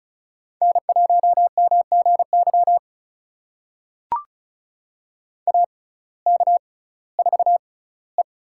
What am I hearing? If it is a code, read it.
Morse code, heard as a keyed tone, about 36 wpm, spaced as for 19 wpm, 700 Hz. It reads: N1MGY AK4E
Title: USA Call Signs - 5 Characters - 6x Character Spacing - 35wpm